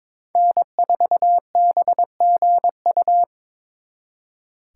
Morse code, heard as a keyed tone, about 26 wpm, 700 Hz. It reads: N4BGU
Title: Speed Racing - USA 1x3 Call Signs 22wpm